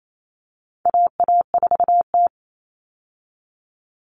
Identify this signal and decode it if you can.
Morse code, heard as a keyed tone, about 28 wpm, 700 Hz. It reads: AA4T
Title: All Cut Numbers - 4-Digit 28wpm